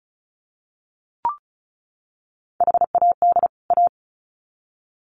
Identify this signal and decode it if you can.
Morse code, heard as a keyed tone, about 34 wpm, 700 Hz. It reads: HAD A
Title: Sets of 2 Words 35wpm